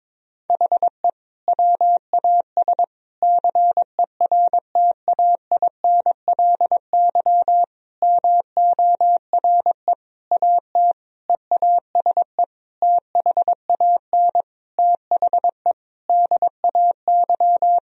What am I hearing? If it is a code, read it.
Morse code, heard as a keyed tone, about 22 wpm, 700 Hz. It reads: HE WAS CERTAINLY MORE AT EASE THAN THE DAY
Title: Sentences from Top 1000 Words - Review Part 1 - 22wpm